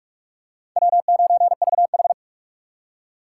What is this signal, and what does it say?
Morse code, heard as a keyed tone, about 44 wpm, 700 Hz. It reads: W9VH